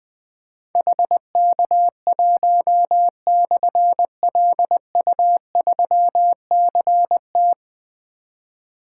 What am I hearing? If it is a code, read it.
Morse code, heard as a keyed tone, about 21 wpm, 700 Hz. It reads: HK1/LU3CT
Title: Difficult Call Signs 20wpm